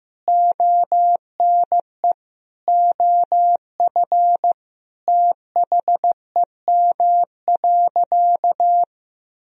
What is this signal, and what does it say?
Morse code, heard as a keyed tone, about 15 wpm, 700 Hz. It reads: ONE OF THEM.
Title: Sentences from Top 300 Words and Abbreviations 15wpm